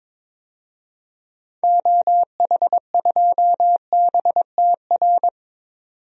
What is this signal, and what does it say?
Morse code, heard as a keyed tone, about 22 wpm, 700 Hz. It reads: OH2BTR
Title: DXCC-100 Call Signs - 6 Characters - 22wpm